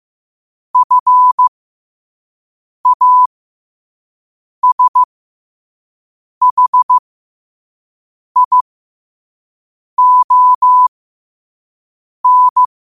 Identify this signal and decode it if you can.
Morse code, heard as a keyed tone, about 15 wpm, spaced as for 5 wpm, 1000 Hz. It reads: FASHION
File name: qotdfiles / 5wpm